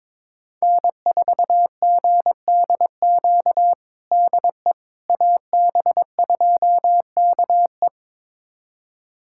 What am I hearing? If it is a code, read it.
Morse code, heard as a keyed tone, about 22 wpm, 700 Hz. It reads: N4GDQ DE AB2KE
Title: USA Callsign DE Callsign - 5 Characters - 22wpm